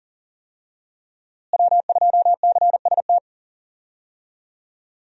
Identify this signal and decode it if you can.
Morse code, heard as a keyed tone, about 40 wpm, 700 Hz. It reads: W2CST
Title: USA Call Signs - 5 Characters - Repeated 2x - 40wpm